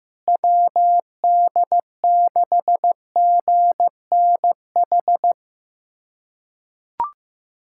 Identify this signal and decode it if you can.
Morse code, heard as a keyed tone, about 16 wpm, 700 Hz. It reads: WD6GNH
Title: USA Callsign DE Callsign 15wpm